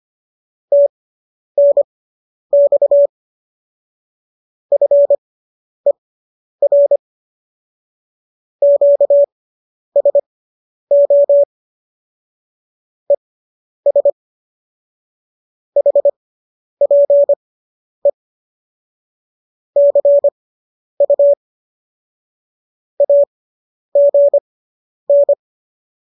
Morse code, heard as a keyed tone, about 25 wpm, spaced as for 10 wpm, 570 Hz. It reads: TNX FER QSO ES HPE CU AGN